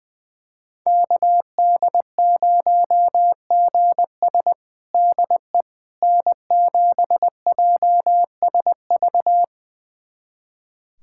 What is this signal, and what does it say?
Morse code, heard as a keyed tone, about 21 wpm, 700 Hz. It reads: KD0GS DE N7JSV
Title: USA Callsign DE Callsign - 5 Characters - 20wpm